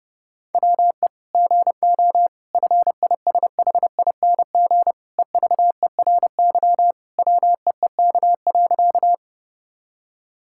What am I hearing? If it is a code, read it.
Morse code, heard as a keyed tone, about 32 wpm, 700 Hz. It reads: WE GO FISHING EVERY WEEK.
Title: Sentences from Top 1000 Words - Review Part 5 - 30wpm